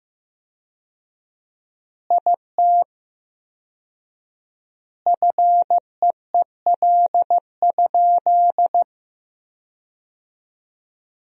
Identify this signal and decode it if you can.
Morse code, heard as a keyed tone, about 15 wpm, 700 Hz. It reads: IT FEEL?
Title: 4x Word-Spacing - Sentences from Top 400 Words - 15wpm